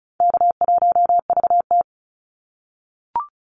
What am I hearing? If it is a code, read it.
Morse code, heard as a keyed tone, about 36 wpm, 700 Hz. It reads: K1VT
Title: Call Signs - 35wpm